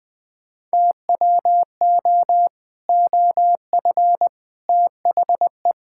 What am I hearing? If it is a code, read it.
Morse code, heard as a keyed tone, about 21 wpm, 700 Hz. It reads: TWO OF THE
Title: Sentences with 3-Word Phrases - 20wpm